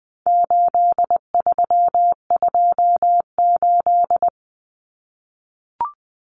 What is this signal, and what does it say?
Morse code, heard as a keyed tone, about 20 wpm, 700 Hz. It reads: 8328
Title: Numbers - 4-Digit 20wpm